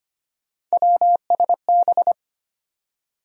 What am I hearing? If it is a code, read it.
Morse code, heard as a keyed tone, about 25 wpm, 700 Hz. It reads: WSB
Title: USA Call Sign Postfixes - 3 Letters 25wpm